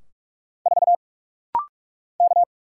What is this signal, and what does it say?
Morse code, heard as a keyed tone, about 44 wpm, 700 Hz. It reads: 4 K
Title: Single Letter-Number Alt - Rapid-Fire 45wpm